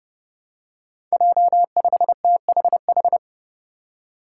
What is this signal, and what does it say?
Morse code, heard as a keyed tone, about 30 wpm, 700 Hz. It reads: J5THH
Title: Five Challenging Letter-Number Combinations 30wpm